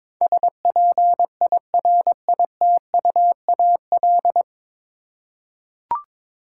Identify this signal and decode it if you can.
Morse code, heard as a keyed tone, about 22 wpm, 700 Hz. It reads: SPIRITUAL
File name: practice / 22wpm / 9 Letter Words - First 3 Letters Given - 22wpm